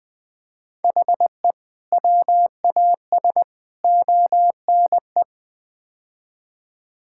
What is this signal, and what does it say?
Morse code, heard as a keyed tone, about 20 wpm, 700 Hz. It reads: HE WAS ONE
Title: Build A Phrase - 5-Word Phrases - 20wpm